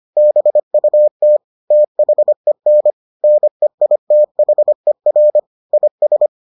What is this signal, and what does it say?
Morse code, heard as a keyed tone, about 25 wpm, 600 Hz. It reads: BUT THEN NEITHER IS